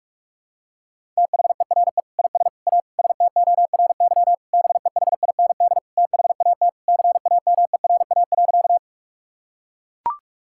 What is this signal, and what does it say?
Morse code, heard as a keyed tone, about 41 wpm, 700 Hz. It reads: THERE IS A STORY BEHIND THAT CAMERA.
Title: Sentences from Top 1000 Words - Review Part 5 - 45wpm